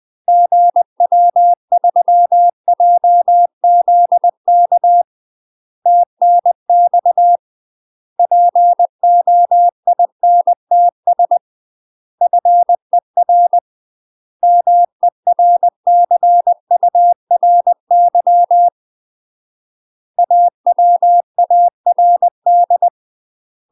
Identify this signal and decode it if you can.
Morse code, heard as a keyed tone, about 20 wpm, 700 Hz. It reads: GW3JZK TNX POINTS FER MERCURY AWARD
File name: UK MASTER QSOs TRAINING / 20WPM / UK MASTER QSOs 0038